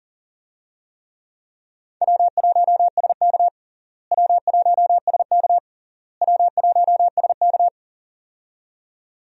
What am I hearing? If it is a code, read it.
Morse code, heard as a keyed tone, about 40 wpm, 700 Hz. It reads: W1SK W1SK W1SK
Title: USA Call Signs - 4 Characters - Repeated 3x - 40wpm